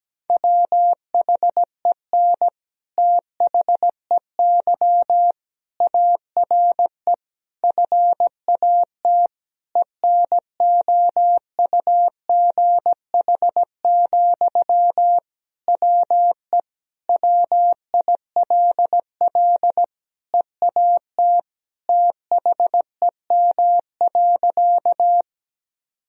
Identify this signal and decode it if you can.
Morse code, heard as a keyed tone, about 17 wpm, 700 Hz. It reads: WHEN THEY ARE FAT ENOUGH, WE WILL EAT THEM.
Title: Sentences from Top 1000 Words - Review Part 4 - 17wpm